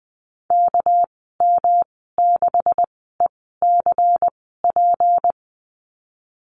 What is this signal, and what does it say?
Morse code, heard as a keyed tone, about 19 wpm, spaced as for 14 wpm, 700 Hz. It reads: KM6ECP
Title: 219 Lesson 35g - Number 0 - Call Signs 20wpm